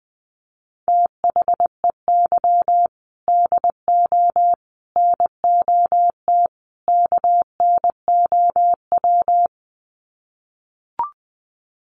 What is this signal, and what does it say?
Morse code, heard as a keyed tone, about 19 wpm, 700 Hz. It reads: THEY DO NOT KNOW
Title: Sets of 4 Words 20wpm